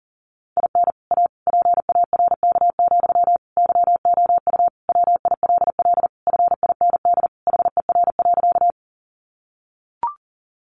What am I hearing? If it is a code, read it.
Morse code, heard as a keyed tone, about 40 wpm, 700 Hz. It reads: IN A PARK, YOU WILL FIND HER.
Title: Sentences from Top 1300 Words 40wpm